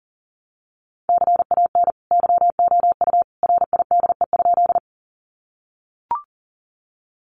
Morse code, heard as a keyed tone, about 40 wpm, 700 Hz. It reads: CAN YOU RIDE?
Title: Sentences from Top 1200 Words 40wpm